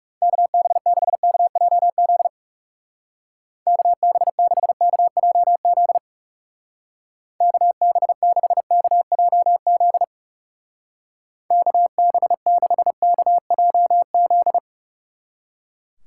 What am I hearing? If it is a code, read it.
Morse code, heard as a keyed tone, about 37 wpm, 700 Hz. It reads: KB6KJZ KB6KJZ KB6KJZ KB6KJZ
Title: Speed Racing - USA 2x3 Call Signs 30wpm